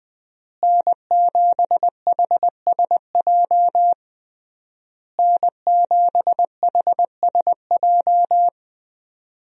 Morse code, heard as a keyed tone, about 21 wpm, 700 Hz. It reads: N7HSJ N7HSJ
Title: USA Call Signs - 5 Characters - Repeated 3x - 20wpm